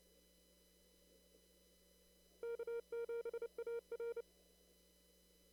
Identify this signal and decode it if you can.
Morse code, heard as a keyed tone, about 28 wpm, 470 Hz. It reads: K7AR
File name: CW / kt5j wpx cw-op-k5pi